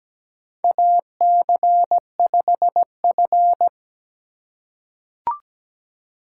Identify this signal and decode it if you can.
Morse code, heard as a keyed tone, about 17 wpm, 700 Hz. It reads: AC5F
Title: USA Call Signs - 2x Character Spacing - 17wpm